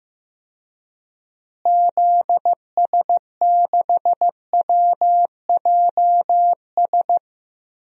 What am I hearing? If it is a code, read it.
Morse code, heard as a keyed tone, about 15 wpm, 700 Hz. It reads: ZS6WJS
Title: DXCC-100 Call Signs - 6 Characters - Repeated 3x - 15wpm